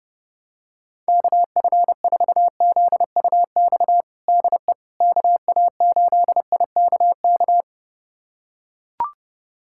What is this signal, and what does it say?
Morse code, heard as a keyed tone, about 30 wpm, 700 Hz. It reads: KF4ZUX DE KA8IKK
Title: USA Callsign DE Callsign - 6 Characters - 30wpm